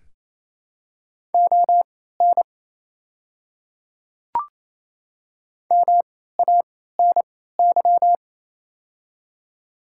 Morse code, heard as a keyed tone, about 27 wpm, spaced as for 16 wpm, 700 Hz. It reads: ON MANY
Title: 3x Character Spacing - Top 100 Words - 28wpm